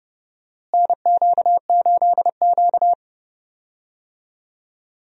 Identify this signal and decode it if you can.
Morse code, heard as a keyed tone, about 30 wpm, 700 Hz. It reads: NQ8Q
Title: USA Call Signs - 4 Characters - Repeated 2x - 30wpm